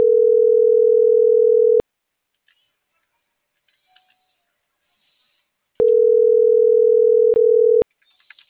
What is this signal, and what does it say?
Ringback tone (North America), 440 + 480 Hz.